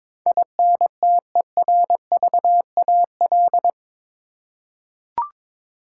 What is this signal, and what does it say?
Morse code, heard as a keyed tone, about 22 wpm, 700 Hz. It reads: INTERVAL
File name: practice / 22wpm / Eight Letter Words - 22wpm